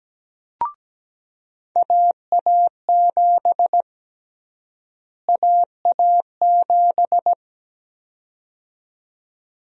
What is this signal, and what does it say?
Morse code, heard as a keyed tone, about 17 wpm, 700 Hz. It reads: AA7 AA7
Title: USA Call Sign Prefixes - Repeated 2x - 17wpm